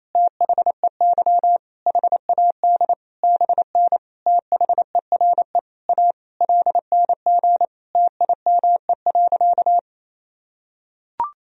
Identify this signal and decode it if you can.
Morse code, heard as a keyed tone, about 28 wpm, 700 Hz. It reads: THEY HAD BN THERE A LNG TIME.
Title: Sentences from Top 400 Words and Abbreviations 28wpm